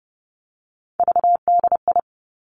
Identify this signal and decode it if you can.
Morse code, heard as a keyed tone, about 30 wpm, 700 Hz. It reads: VDI
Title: Three Challenging Letter-Number Combinations 30wpm